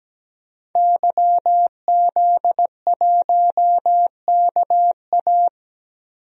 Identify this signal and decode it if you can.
Morse code, heard as a keyed tone, about 17 wpm, 700 Hz. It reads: YZ1KA
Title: DXCC-100 Call Signs - 5 Characters - 17wpm